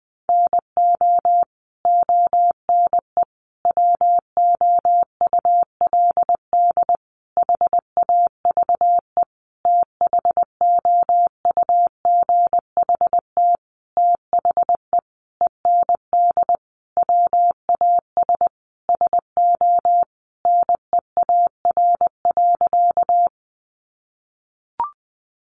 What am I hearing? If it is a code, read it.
Morse code, heard as a keyed tone, about 20 wpm, 700 Hz. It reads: NO ONE WOULD HAVE THOUGHT THE END WAS SO NEAR.
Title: Sentences from Top 300 Words 20wpm